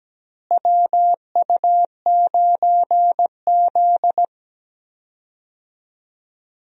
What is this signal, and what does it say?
Morse code, heard as a keyed tone, about 17 wpm, 700 Hz. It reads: WU9Z
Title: USA Call Signs - 4 Characters - Repeated 3x - 17wpm